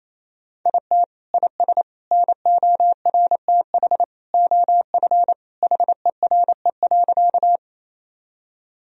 Morse code, heard as a keyed tone, about 28 wpm, 700 Hz. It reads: IT IS NORTH OF HERE.
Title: Sentences from Top 1000 Words - Review Part 1 - 28wpm